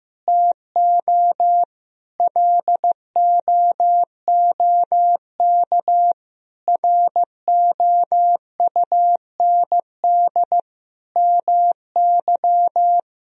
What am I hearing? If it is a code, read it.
Morse code, heard as a keyed tone, about 15 wpm, 700 Hz. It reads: TO LOOK ROUND MY
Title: Sentences from Top 200 Words and Abbreviations 15wpm